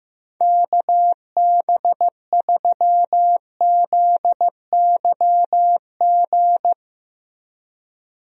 Morse code, heard as a keyed tone, about 15 wpm, 700 Hz. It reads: KB3ZYG
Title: USA Call Signs - 6 Characters - Repeated 3x - 15wpm